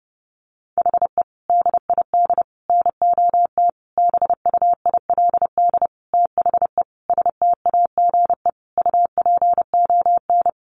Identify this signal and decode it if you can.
Morse code, heard as a keyed tone, about 30 wpm, 700 Hz. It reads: HE DID NOT BUILD THE STAGE UPON
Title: Sentences from Top 1200 Words 30wpm